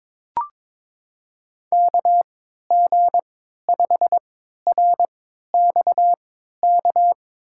Morse code, heard as a keyed tone, about 22 wpm, spaced as for 12 wpm, 700 Hz. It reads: KG5RXK
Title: USA Call Signs - 3x Character Spacing - 22wpm